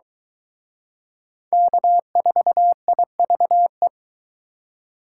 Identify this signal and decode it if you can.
Morse code, heard as a keyed tone, about 24 wpm, 700 Hz. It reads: K4IVE